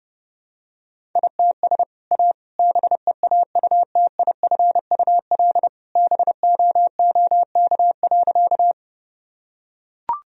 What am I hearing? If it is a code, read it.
Morse code, heard as a keyed tone, about 30 wpm, 700 Hz. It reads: ITS A BEAUTIFUL BOOK.
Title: Sentences from Top 1000 Words - Review Part 1 - 30wpm